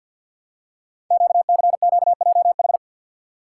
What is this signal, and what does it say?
Morse code, heard as a keyed tone, about 48 wpm, 700 Hz. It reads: QCQJH
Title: Code Groups - 5 Letters - 50wpm